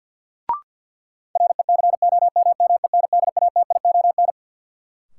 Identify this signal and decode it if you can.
Morse code, heard as a keyed tone, about 52 wpm, 700 Hz. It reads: RECOMMENDATION
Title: Sentences from Top 1400 Words - New Word Drill - 50wpm